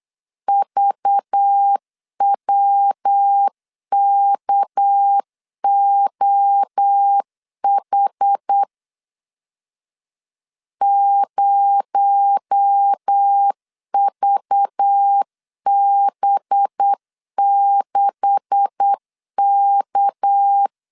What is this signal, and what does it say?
Morse code, heard as a keyed tone, about 8 wpm, 800 Hz. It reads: VWKOH 0VB6K
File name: CW / DET / 1 9 1500C S C 8wpm 5653